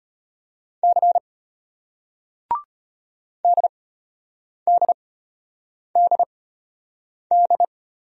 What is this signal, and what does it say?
Morse code, heard as a keyed tone, about 34 wpm, spaced as for 8 wpm, 700 Hz. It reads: C DDDD